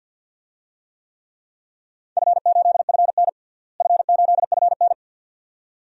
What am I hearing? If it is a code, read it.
Morse code, heard as a keyed tone, about 51 wpm, 700 Hz. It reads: F8FN F8FN
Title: DXCC-100 Call Signs - 4 Characters - Repeated 2x - 50wpm